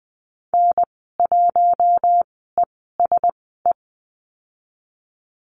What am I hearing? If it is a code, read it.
Morse code, heard as a keyed tone, about 19 wpm, spaced as for 14 wpm, 700 Hz. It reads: N1ESE
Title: 055 Lesson 10g - Letter r - Call Signs 20wpm